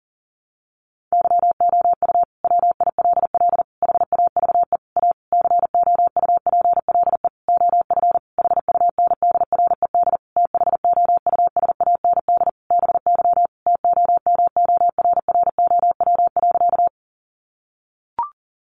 Morse code, heard as a keyed tone, about 37 wpm, 700 Hz. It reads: YOU WILL HAVE A COUPLE OF HUNDRED THOUSAND BY TOMORROW.
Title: Sentences from Top 1100 Words 40wpm